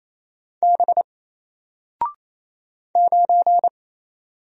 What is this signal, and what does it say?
Morse code, heard as a keyed tone, about 28 wpm, 700 Hz. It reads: B 9
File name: Single-Letter-Number-28wpm